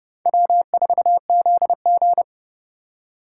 Morse code, heard as a keyed tone, about 30 wpm, 700 Hz. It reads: W4ZG
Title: USA Call Signs - 4 Characters - 2x Character Spacing - 30wpm